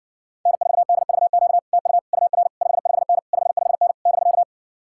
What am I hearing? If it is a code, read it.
Morse code, heard as a keyed tone, about 60 wpm, 700 Hz. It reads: N4DVQ TU UR 55N 55N <BK>